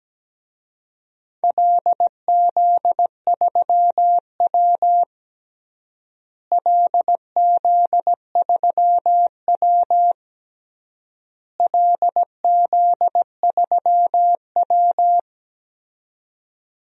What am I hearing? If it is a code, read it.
Morse code, heard as a keyed tone, about 17 wpm, 700 Hz. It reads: LZ3W LZ3W LZ3W
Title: DXCC-100 Call Signs - 4 Characters - Repeated 3x - 17wpm